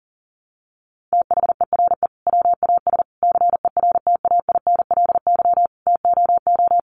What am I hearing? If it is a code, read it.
Morse code, heard as a keyed tone, about 40 wpm, 700 Hz. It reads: THERE WAS CERTAINLY TOO
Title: Sentences from Top 900 Words 40wpm